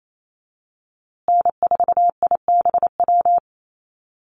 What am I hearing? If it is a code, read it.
Morse code, heard as a keyed tone, about 28 wpm, 700 Hz. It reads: N4IBW
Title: USA Call Signs Repeated 3x 28wpm